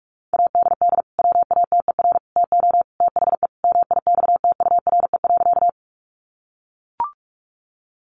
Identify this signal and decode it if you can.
Morse code, heard as a keyed tone, about 44 wpm, 700 Hz. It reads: ADD WATER TO THE MIXTURE.
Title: Sentences from Top 1500 Words 45wpm